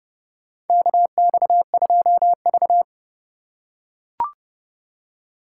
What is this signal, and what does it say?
Morse code, heard as a keyed tone, about 31 wpm, 700 Hz. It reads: KX2V
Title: USA Call Signs - 3x Character Spacing - 30wpm